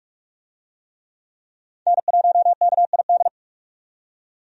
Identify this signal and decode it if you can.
Morse code, heard as a keyed tone, about 46 wpm, 700 Hz. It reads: N1KID
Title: USA Call Signs - 5 Characters - 45wpm